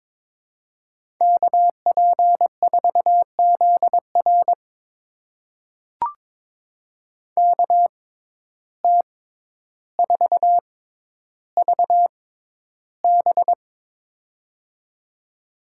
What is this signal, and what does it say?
Morse code, heard as a keyed tone, about 22 wpm, spaced as for 11 wpm, 700 Hz. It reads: KP4ZR KT4VB
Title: USA Call Signs - 5 Characters - 6x Character Spacing - 22wpm